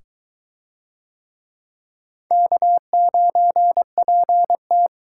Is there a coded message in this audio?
Yes: Morse code from 2.3 to 4.9 s, heard as a keyed tone, about 22 wpm, 700 Hz. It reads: K9PT